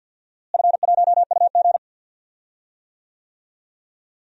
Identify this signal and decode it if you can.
Morse code, heard as a keyed tone, about 50 wpm, 700 Hz. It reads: F1UG
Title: DXCC-100 Call Signs - 4 Characters - Repeated 3x - 50wpm